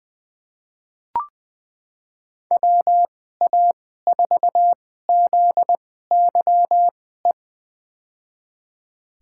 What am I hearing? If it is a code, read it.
Morse code, heard as a keyed tone, about 20 wpm, spaced as for 15 wpm, 700 Hz. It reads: WA4ZYE
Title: USA Call Signs - 6 Characters - 2x Character Spacing - 20wpm